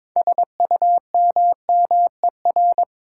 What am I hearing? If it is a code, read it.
Morse code, heard as a keyed tone, about 22 wpm, 700 Hz. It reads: SUMMER
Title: Sentences from Top 1100 Words - New Word Drill Alt - 22wpm